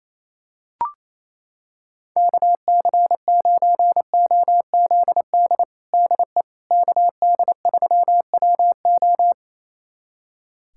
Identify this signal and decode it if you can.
Morse code, heard as a keyed tone, about 28 wpm, 700 Hz. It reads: KC9OZD DE KD3WO
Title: USA Callsign DE Callsign 28wpm